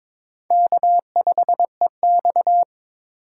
Morse code, heard as a keyed tone, about 26 wpm, 700 Hz. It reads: K5EX
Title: Speed Racing - USA 1x2 Call Signs 22wpm